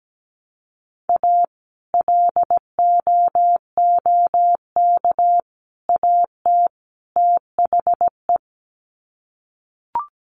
Morse code, heard as a keyed tone, about 17 wpm, 700 Hz. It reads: A LOOK AT THE
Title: Sets of 4 Words 17wpm